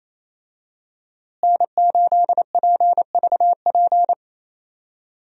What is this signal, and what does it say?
Morse code, heard as a keyed tone, about 28 wpm, 700 Hz. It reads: N8PVP
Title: USA Call Signs - 5 Characters - Repeated 3x - 28wpm